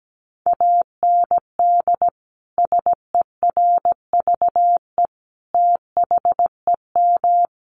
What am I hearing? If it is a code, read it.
Morse code, heard as a keyed tone, about 17 wpm, 700 Hz. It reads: AND SERVE THEM
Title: Sentences from Top 400 Words 17wpm